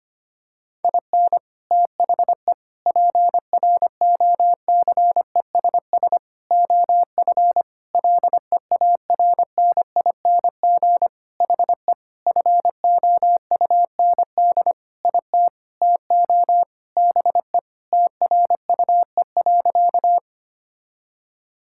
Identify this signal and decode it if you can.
Morse code, heard as a keyed tone, about 25 wpm, 700 Hz. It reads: IN THE PROCESS OF LEARNING HE FOUND IT TO BE TRUE.
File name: practice / 25wpm / Sentences with 4-Word Phrases - 25wpm